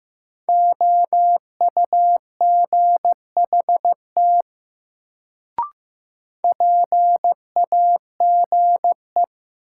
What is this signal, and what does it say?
Morse code, heard as a keyed tone, about 15 wpm, 700 Hz. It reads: OUGHT PAGE
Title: Sets of 5 Words Encore - New Word Drill - 15wpm